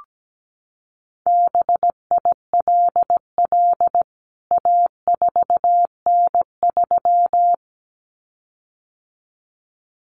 Morse code, heard as a keyed tone, about 17 wpm, 700 Hz. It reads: BILL A4N3
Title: CWT Exchange 17wpm